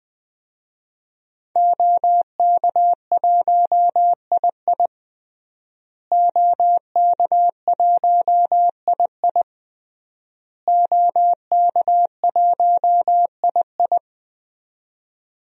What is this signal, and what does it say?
Morse code, heard as a keyed tone, about 20 wpm, 700 Hz. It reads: OK1II OK1II OK1II
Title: DXCC-100 Call Signs - 5 Characters - Repeated 3x - 20wpm